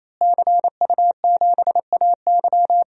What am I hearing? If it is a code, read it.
Morse code, heard as a keyed tone, about 28 wpm, 700 Hz. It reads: CU7AY
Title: DXCC-100 Call Signs - 5 Characters - Repeated 3x - 28wpm